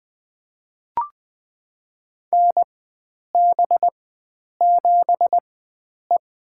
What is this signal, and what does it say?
Morse code, heard as a keyed tone, about 20 wpm, spaced as for 9 wpm, 700 Hz. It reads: NB7E
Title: USA Call Signs - 4 Characters - 4x Character Spacing - 20wpm